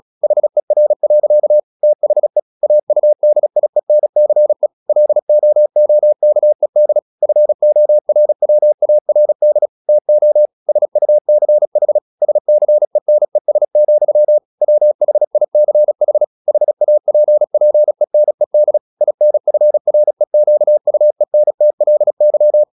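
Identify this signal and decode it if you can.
Morse code, heard as a keyed tone, about 36 wpm, 600 Hz. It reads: HER. THE AUDIENCE LOOKED FORWARD TO SUCH SCENES, WHICH HAPPENED INFREQUENTLY
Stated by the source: Morse practice at 36 wpm, text of Ken Follett's novel Never, from CW med Gnister.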